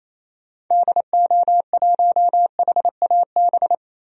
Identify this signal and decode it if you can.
Morse code, heard as a keyed tone, about 28 wpm, 700 Hz. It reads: DO1HAB
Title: DXCC-100 Call Signs - 6 Characters - 28wpm